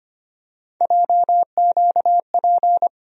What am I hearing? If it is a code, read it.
Morse code, heard as a keyed tone, about 25 wpm, 700 Hz. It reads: JQP
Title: USA Call Sign Postfixes - 3 Letters 25wpm